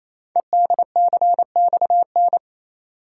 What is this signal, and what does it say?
Morse code, heard as a keyed tone, about 27 wpm, 700 Hz. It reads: EDCXN
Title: Code Groups - 5 Letters - 28wpm